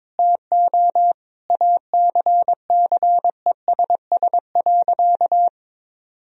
Morse code, heard as a keyed tone, about 22 wpm, 700 Hz. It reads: TO ACCESS.
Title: Sentences from Top 1000 Words - Review Part 4 - 22wpm